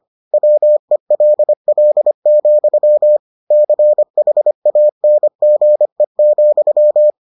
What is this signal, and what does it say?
Morse code, heard as a keyed tone, about 25 wpm, 600 Hz. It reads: WELL, CHANGE,